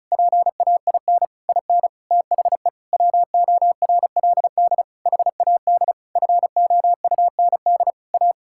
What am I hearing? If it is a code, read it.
Morse code, heard as a keyed tone, about 36 wpm, 700 Hz. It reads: PAIN IN THE WORLD HAD FOUND A
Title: Sentences from Top 1600 Words 35wpm